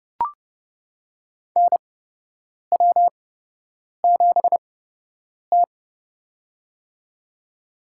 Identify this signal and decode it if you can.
Morse code, heard as a keyed tone, about 32 wpm, spaced as for 8 wpm, 700 Hz. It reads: NW7T
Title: USA Call Signs - 8x Character Spacing - 30wpm